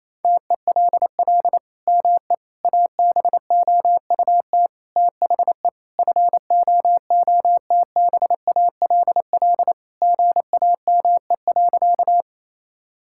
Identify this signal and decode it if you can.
Morse code, heard as a keyed tone, about 28 wpm, 700 Hz. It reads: TELL ME ABOUT THE FOOTBALL GAME.
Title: Sentences from Top 1600 Words 28wpm